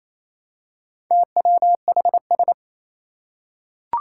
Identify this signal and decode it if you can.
Morse code, heard as a keyed tone, about 27 wpm, 700 Hz. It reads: TWHS
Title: Four Challenging Letter-Number Combinations 28wpm